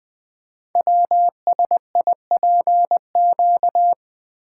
Morse code, heard as a keyed tone, about 20 wpm, 700 Hz. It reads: WSIPQ
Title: Code Groups - 5 Letters - 20wpm